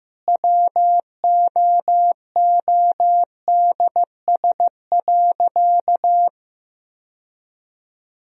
Morse code, heard as a keyed tone, about 16 wpm, 700 Hz. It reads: WOODS.
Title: Sentences from Top 1000 Words - Review Part 2 - 15wpm